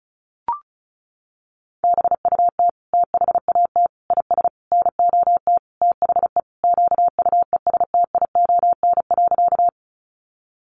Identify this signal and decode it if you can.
Morse code, heard as a keyed tone, about 34 wpm, 700 Hz. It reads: BUT THAT IS NOT THE QUESTION.
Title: Sentences from Top 600 Words 35wpm